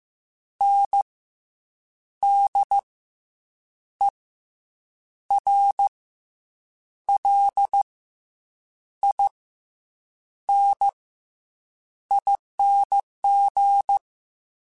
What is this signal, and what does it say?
Morse code, heard as a keyed tone, about 15 wpm, 800 Hz. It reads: N D E R L I N ING
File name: Special INGs / Special INGs-mp3s / SlowToFaster ing @13